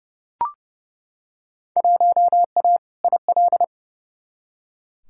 Morse code, heard as a keyed tone, about 30 wpm, 700 Hz. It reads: 1A IL
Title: Field Day Exchanges - 30wpm